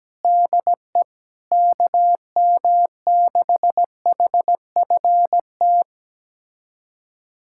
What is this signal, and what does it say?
Morse code, heard as a keyed tone, about 17 wpm, 700 Hz. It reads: DE KM6HFT
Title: USA Callsign DE Callsign - 6 Characters - 17wpm